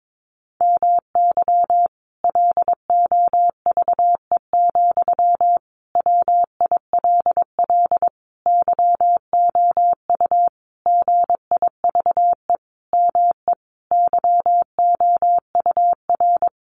Morse code, heard as a keyed tone, about 22 wpm, 700 Hz. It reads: MY LOVE, WILL YOU GIVE ME YOUR
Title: Sentences from Top 1200 Words 22wpm